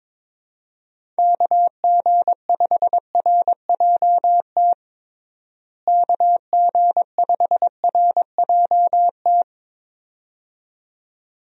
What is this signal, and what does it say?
Morse code, heard as a keyed tone, about 21 wpm, 700 Hz. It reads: KG5RJT KG5RJT
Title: USA Call Signs - 6 Characters - Repeated 2x - 22wpm